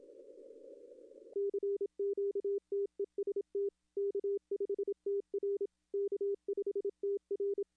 Morse code, heard as a keyed tone, about 26 wpm, 410 Hz. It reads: CQ TEST K5TR K5TR
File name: contests / arrl-160-k5tr-op-w5tn -0800